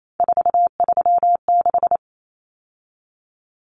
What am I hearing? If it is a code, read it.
Morse code, heard as a keyed tone, about 28 wpm, 700 Hz. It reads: 436
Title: Numbers - 3-Digit 28wpm